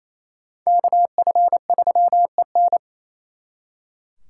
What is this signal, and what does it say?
Morse code, heard as a keyed tone, about 35 wpm, 700 Hz. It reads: KF3EN
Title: Speed Racing - USA 2x2 Call Signs 28wpm